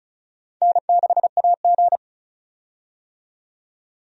Morse code, heard as a keyed tone, about 36 wpm, 700 Hz. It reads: N6AG
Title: USA Call Signs - 4 Characters - 35wpm